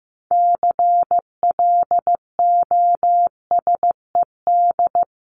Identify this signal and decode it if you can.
Morse code, heard as a keyed tone, about 15 wpm, 700 Hz. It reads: CLOSED
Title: Top 1251-1500 Words 15wpm